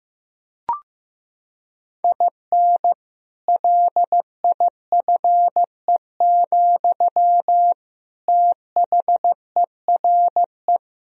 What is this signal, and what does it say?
Morse code, heard as a keyed tone, about 15 wpm, 700 Hz. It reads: IN LIFE, THERE
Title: Sentences from Top 1000 Words - Review Part 4 - 15wpm